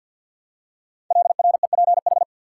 Morse code, heard as a keyed tone, about 50 wpm, 700 Hz. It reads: LREPH
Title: Code Groups - 5 Letters - 50wpm